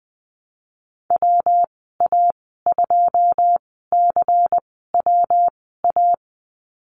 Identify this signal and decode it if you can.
Morse code, heard as a keyed tone, about 20 wpm, spaced as for 14 wpm, 700 Hz. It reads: WA2CWA
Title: 140 Lesson 23g - Letter f - Call Signs 20wpm